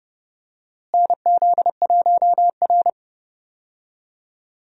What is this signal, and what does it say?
Morse code, heard as a keyed tone, about 30 wpm, 700 Hz. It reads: NZ1R
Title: USA Call Signs - 4 Characters - Repeated 3x - 30wpm